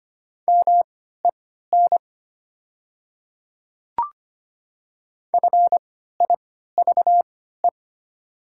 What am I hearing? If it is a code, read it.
Morse code, heard as a keyed tone, about 25 wpm, spaced as for 14 wpm, 700 Hz. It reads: MEN FIVE
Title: 3x Character Spacing - Top 500 Words - 25wpm